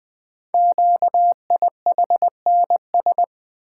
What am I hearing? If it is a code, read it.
Morse code, heard as a keyed tone, about 20 wpm, 700 Hz. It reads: QIHNS
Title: Code Groups - 5 Letters - 20wpm